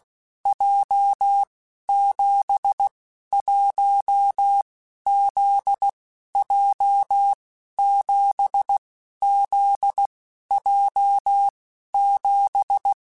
Morse code, heard as a keyed tone, about 16 wpm, 800 Hz. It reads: J 7 1 Z J 7 Z J 7
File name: Basic Letter Groups By7Groups / JZ17 / JZ17@15